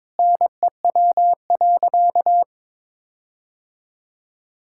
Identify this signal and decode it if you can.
Morse code, heard as a keyed tone, about 22 wpm, 700 Hz. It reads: NEW.
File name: Sentences with 9-Word Phrases - 22wpm